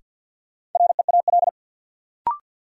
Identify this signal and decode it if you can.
Morse code, heard as a keyed tone, about 47 wpm, 700 Hz. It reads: REAL
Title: Sentences from Top 300 Words - New Word Drill - 50wpm